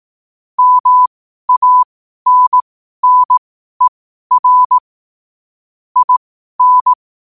Morse code, heard as a keyed tone, about 18 wpm, spaced as for 12 wpm, 1000 Hz. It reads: MANNER IN
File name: qotdfiles / 12wpm